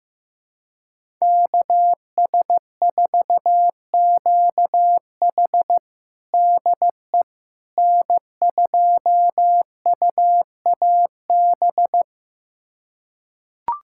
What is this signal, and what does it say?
Morse code, heard as a keyed tone, about 16 wpm, 700 Hz. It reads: KS4QH DE N2UAB
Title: USA Callsign DE Callsign - 5 Characters - 15wpm